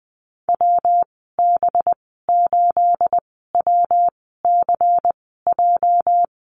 Morse code, heard as a keyed tone, about 20 wpm, spaced as for 14 wpm, 700 Hz. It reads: WB8WCJ
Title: 219 Lesson 35g - Number 0 - Call Signs 20wpm